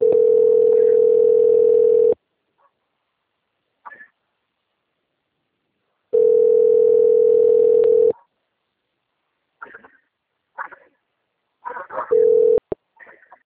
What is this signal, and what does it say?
Ringback tone (North America), 440 + 480 Hz.
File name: out- - - - - .45974